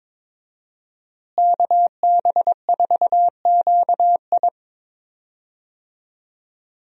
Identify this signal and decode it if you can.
Morse code, heard as a keyed tone, about 22 wpm, 700 Hz. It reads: KB4QI
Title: USA Call Signs - 5 Characters - Repeated 3x - 22wpm